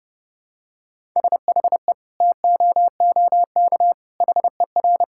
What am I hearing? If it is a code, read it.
Morse code, heard as a keyed tone, about 30 wpm, 700 Hz. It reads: SHE TOOK HER
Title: Sentences from Top 1000 Words - Review Part 5 - 30wpm